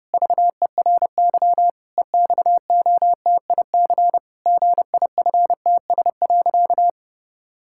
Morse code, heard as a keyed tone, about 30 wpm, 700 Hz. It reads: VERY EXOTIC GIFTS.
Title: Pangram Challenge v2 - 30wpm